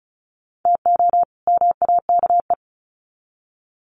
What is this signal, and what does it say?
Morse code, heard as a keyed tone, about 34 wpm, 700 Hz. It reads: TO MAKE
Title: Sets of 2 Words 35wpm